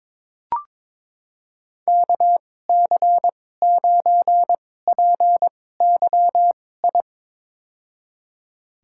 Morse code, heard as a keyed tone, about 22 wpm, spaced as for 16 wpm, 700 Hz. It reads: KC9PYI
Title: USA Call Signs - 6 Characters - 2x Character Spacing - 22wpm